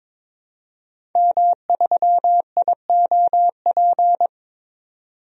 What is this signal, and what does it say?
Morse code, heard as a keyed tone, about 22 wpm, 700 Hz. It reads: M3IOP
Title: DXCC-100 Call Signs - 5 Characters - 22wpm